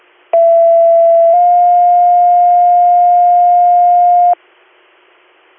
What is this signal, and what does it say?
Two-tone page, heard as a one-second tone, then a three-second tone, about 670 then 710 Hz. It is heard from 0.3 to 4.3 s.